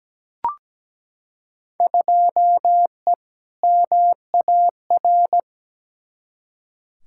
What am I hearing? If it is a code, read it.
Morse code, heard as a keyed tone, about 17 wpm, 700 Hz. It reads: 2E MAR
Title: Field Day Exchanges Encore - 17wpm